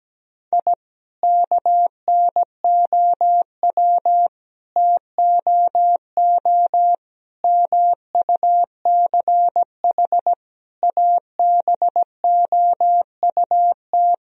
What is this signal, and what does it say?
Morse code, heard as a keyed tone, about 17 wpm, 700 Hz. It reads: I KNOW TOO MUCH ABOUT
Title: Sentences from Top 1000 Words - Review Part 3 - 17wpm